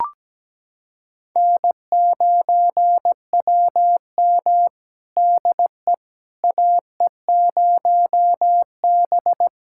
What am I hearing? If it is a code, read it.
Morse code, heard as a keyed tone, about 17 wpm, 700 Hz. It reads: N9WM DE AE0B
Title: USA Callsign DE Callsign - 4 Characters - 17wpm